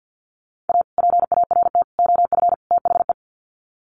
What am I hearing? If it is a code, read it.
Morse code, heard as a keyed tone, about 50 wpm, 700 Hz. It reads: A PART OF THE
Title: Sets of 4 Words 50wpm